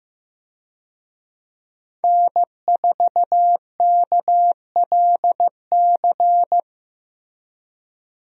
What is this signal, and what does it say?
Morse code, heard as a keyed tone, about 15 wpm, 700 Hz. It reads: N4KLC
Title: USA Call Signs - 5 Characters - Repeated 2x - 15wpm